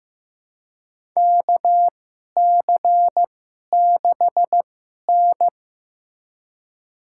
Morse code, heard as a keyed tone, about 15 wpm, spaced as for 11 wpm, 690 Hz. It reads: KC6N
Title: USA Call Signs - 4 Characters - 2x Character Spacing - 15wpm